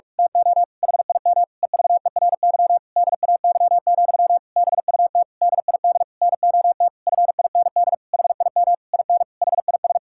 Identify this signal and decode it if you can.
Morse code, heard as a keyed tone, about 46 wpm, 700 Hz. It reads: TO HIM EVERY DAY, BUT DID NOT FIND HIM IN HIS
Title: Sentences from Top 1600 Words 45wpm